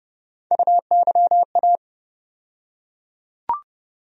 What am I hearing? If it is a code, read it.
Morse code, heard as a keyed tone, about 30 wpm, 700 Hz. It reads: UYA
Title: USA Call Sign Postfixes - 3 Letters 30wpm